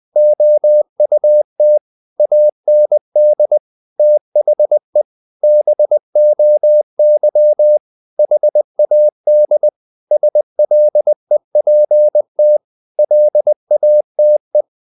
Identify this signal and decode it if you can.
Morse code, heard as a keyed tone, about 20 wpm, 600 Hz. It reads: OUT AND THE BOY HAD SLEPT LATE
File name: the old man and the sea part 2 20wpm